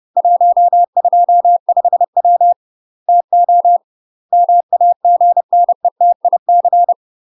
Morse code, heard as a keyed tone, about 30 wpm, 700 Hz. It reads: 125W TO MAGNETIC
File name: UK MASTER QSOs TRAINING / 30WPM / UK MASTER QSOs 0000